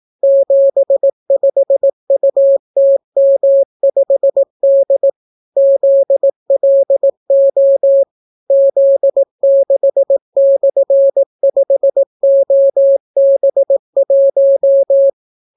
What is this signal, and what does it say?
Morse code, heard as a keyed tone, about 18 wpm, 550 Hz. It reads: 75UTM5D ZLO Z6/5OB1